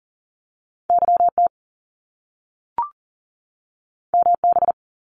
Morse code, heard as a keyed tone, about 41 wpm, 700 Hz. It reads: YT MB
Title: Canadian Provinces and Territories 40wpm